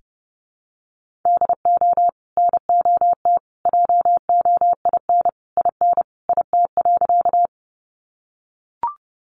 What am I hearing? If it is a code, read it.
Morse code, heard as a keyed tone, about 30 wpm, 700 Hz. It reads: DO NOT JOIN IN IT.
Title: Sentences from Top 1400 Words 30wpm